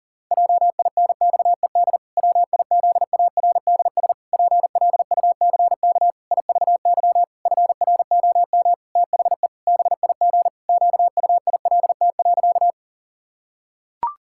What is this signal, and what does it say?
Morse code, heard as a keyed tone, about 40 wpm, 700 Hz. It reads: JINXED WIZARDS PLUCK IVY FROM THE BIG QUILT.
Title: Pangram Challenge v2 - 40wpm